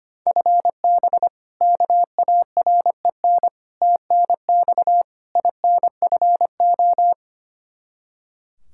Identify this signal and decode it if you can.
Morse code, heard as a keyed tone, about 25 wpm, 700 Hz. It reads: FB KAREN TNX INFO
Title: K1USN SST QSO Practice 25wpm